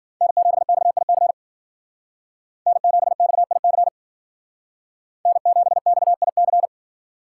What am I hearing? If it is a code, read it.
Morse code, heard as a keyed tone, about 52 wpm, 700 Hz. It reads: N7XIC N7XIC N7XIC